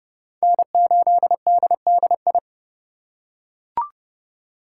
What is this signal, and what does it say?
Morse code, heard as a keyed tone, about 30 wpm, 700 Hz. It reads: N8DDI